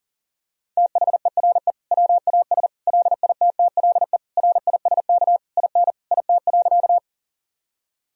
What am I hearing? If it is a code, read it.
Morse code, heard as a keyed tone, about 40 wpm, 700 Hz. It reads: THERE WAS LITTLE RISK IN IT.
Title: Sentences from Top 1000 Words - Review Part 4 - 40wpm